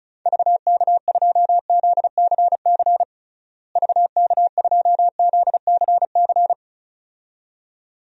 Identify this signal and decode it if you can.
Morse code, heard as a keyed tone, about 36 wpm, 700 Hz. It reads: VK2ZCC VK2ZCC
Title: DXCC-100 Call Signs - 6 Characters - Repeated 2x - 35wpm